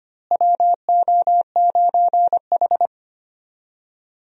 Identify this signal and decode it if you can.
Morse code, heard as a keyed tone, about 25 wpm, 700 Hz. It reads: WO9H
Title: USA Call Signs - 4 Characters - Repeated 3x - 25wpm